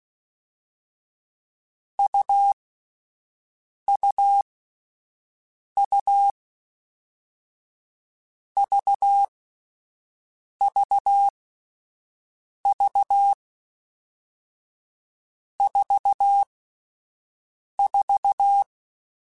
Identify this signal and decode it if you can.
Morse code, heard as a keyed tone, about 16 wpm, 800 Hz. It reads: U U U V V V 4 4
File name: Difficult Combos / Difficult Combos-mp3s / AUV AUV4 @10